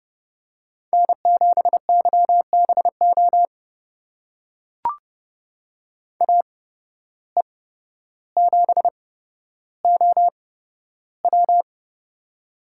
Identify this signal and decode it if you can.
Morse code, heard as a keyed tone, about 30 wpm, spaced as for 13 wpm, 700 Hz. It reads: N7YBO AE7OW
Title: USA Call Signs - 5 Characters - 8x Character Spacing - 30wpm